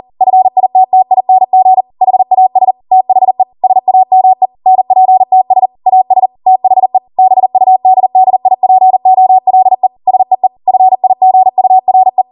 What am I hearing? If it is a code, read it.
Morse code, heard as a keyed tone, about 40 wpm, 750 Hz. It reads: FITTING HAS THE SAME NPTS AS THE BUDDIPOLE SEE FIGURE
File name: Morse / 40WPM